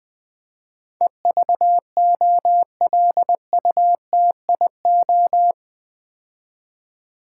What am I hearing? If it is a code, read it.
Morse code, heard as a keyed tone, about 20 wpm, 700 Hz. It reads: EVOLUTIO
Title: Build A Word - 4 Letters At A Time - 12-Letter Words - 20wpm